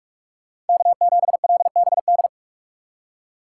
Morse code, heard as a keyed tone, about 59 wpm, 700 Hz. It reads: K7LBD